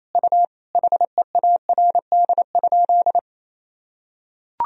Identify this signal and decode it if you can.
Morse code, heard as a keyed tone, about 28 wpm, 700 Hz. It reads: U HEARD?
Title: Sentences from Top 400 Words and Abbreviations 28wpm